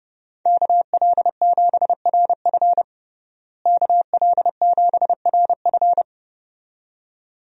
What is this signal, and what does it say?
Morse code, heard as a keyed tone, about 30 wpm, 700 Hz. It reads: KL7RF KL7RF
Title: DXCC-100 Call Signs - 5 Characters - Repeated 2x - 30wpm